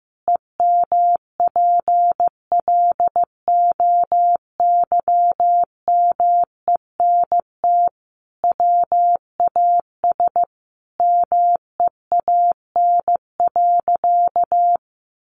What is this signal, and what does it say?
Morse code, heard as a keyed tone, about 15 wpm, 700 Hz. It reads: EMPLOYMENT WAS MEAN.
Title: Sentences from Top 1500 Words 15wpm